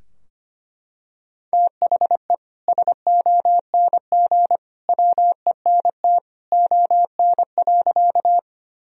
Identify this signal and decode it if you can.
Morse code, heard as a keyed tone, about 25 wpm, 700 Hz. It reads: THE SONG WENT ON.
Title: Sentences from Top 1000 Words - Review Part 2 - 25wpm